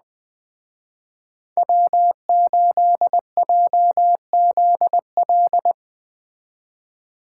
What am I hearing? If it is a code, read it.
Morse code, heard as a keyed tone, about 20 wpm, 700 Hz. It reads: W8JZL
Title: Speed Racing - USA 1x3 Call Signs 17wpm